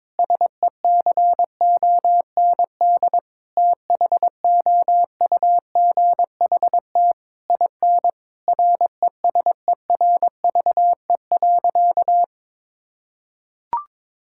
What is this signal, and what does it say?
Morse code, heard as a keyed tone, about 22 wpm, 700 Hz. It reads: SECOND THOUGHT IN RESERVE.
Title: Sentences from Top 1000 Words - Review Part 6 - 22wpm